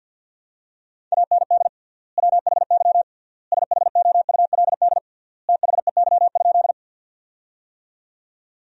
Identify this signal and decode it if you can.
Morse code, heard as a keyed tone, about 51 wpm, 700 Hz. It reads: AND WHY SHOULD THEY?
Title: 3x Word-Spacing - Sentences from Top 300 Words - 50wpm